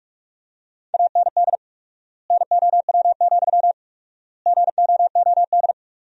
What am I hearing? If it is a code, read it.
Morse code, heard as a keyed tone, about 47 wpm, 700 Hz. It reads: AND NOW, GOOD
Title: 4x Word-Spacing - Sentences from Top 300 Words - 45wpm